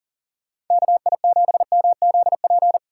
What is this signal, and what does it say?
Morse code, heard as a keyed tone, about 50 wpm, 700 Hz. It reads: KI7MZP
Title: Speed Racing - USA 2x3 Call Signs 40wpm